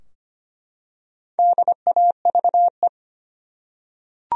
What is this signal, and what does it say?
Morse code, heard as a keyed tone, about 25 wpm, 700 Hz. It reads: DAVE
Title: USA Common Names - Max 4 Letters - 25wpm